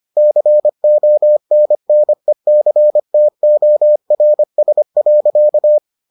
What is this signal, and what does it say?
Morse code, heard as a keyed tone, about 25 wpm, 600 Hz. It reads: CONNECTORS.